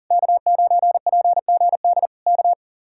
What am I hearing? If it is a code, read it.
Morse code, heard as a keyed tone, about 40 wpm, 700 Hz. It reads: K9PGD K
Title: Speed-Racing Common QSO Phrases - 40wpm